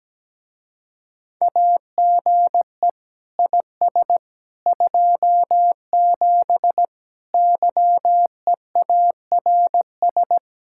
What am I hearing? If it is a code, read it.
Morse code, heard as a keyed tone, about 17 wpm, 700 Hz. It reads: AGE IS 27 YEARS
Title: Speed-Racing Common QSO Phrases Encore - 17wpm